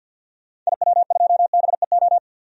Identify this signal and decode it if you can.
Morse code, heard as a keyed tone, about 50 wpm, 700 Hz. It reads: IW2BEO
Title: DXCC-100 Call Signs - 6 Characters - 50wpm